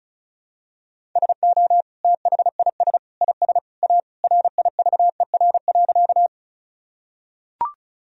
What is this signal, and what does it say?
Morse code, heard as a keyed tone, about 34 wpm, 700 Hz. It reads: SO THIS IS A RIVER.
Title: Sentences from Top 1000 Words - Review Part 1 - 35wpm